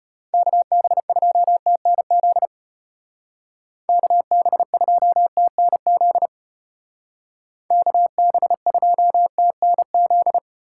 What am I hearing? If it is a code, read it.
Morse code, heard as a keyed tone, about 30 wpm, 700 Hz. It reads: KB2TNZ KB2TNZ KB2TNZ
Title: Speed Racing - USA 2x3 Call Signs 25wpm